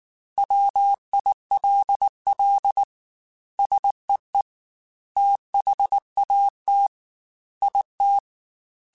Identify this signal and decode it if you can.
Morse code, heard as a keyed tone, about 19 wpm, 800 Hz. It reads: WILL SEE THAT IT
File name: Technical Subjects / Technical Subjects-mp3s / OR Gate @18